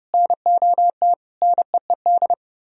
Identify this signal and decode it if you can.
Morse code, heard as a keyed tone, about 33 wpm, 700 Hz. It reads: NOT NEED